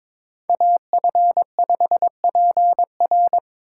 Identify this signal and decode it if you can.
Morse code, heard as a keyed tone, about 22 wpm, 700 Hz. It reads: AF5PR
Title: USA Call Signs - 3x Character Spacing - 22wpm